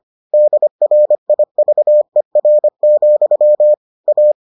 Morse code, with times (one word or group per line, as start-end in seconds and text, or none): 0.3-3.7 DRIVER,
4.1-4.3 A